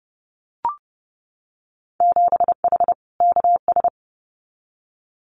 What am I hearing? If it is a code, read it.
Morse code, heard as a keyed tone, about 30 wpm, 700 Hz. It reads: 7H KS
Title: Winter Field Day Exchanges Encore 30wpm